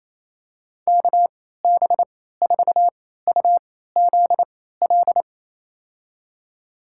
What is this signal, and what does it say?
Morse code, heard as a keyed tone, about 28 wpm, spaced as for 16 wpm, 700 Hz. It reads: KB4UZL
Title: USA Call Signs - 3x Character Spacing - 28wpm